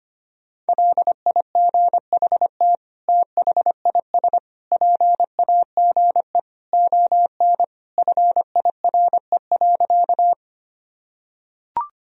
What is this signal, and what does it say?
Morse code, heard as a keyed tone, about 25 wpm, 700 Hz. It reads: LIGHT THIS PAGE ON FIRE.
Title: Sentences from Top 1000 Words - Review Part 1 - 25wpm